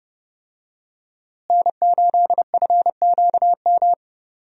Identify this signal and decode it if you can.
Morse code, heard as a keyed tone, about 32 wpm, 700 Hz. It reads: N8FQM
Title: USA Call Signs - 5 Characters - Repeated 2x - 30wpm